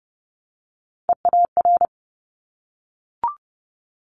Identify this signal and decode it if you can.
Morse code, heard as a keyed tone, about 30 wpm, 700 Hz. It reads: EAR
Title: Top Three Letter Combinations in Words 30wpm